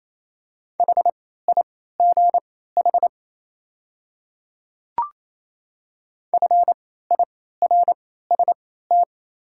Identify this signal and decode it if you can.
Morse code, heard as a keyed tone, about 28 wpm, spaced as for 16 wpm, 700 Hz. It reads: HIGH FIRST
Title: 3x Character Spacing - Top 300 Words - 28wpm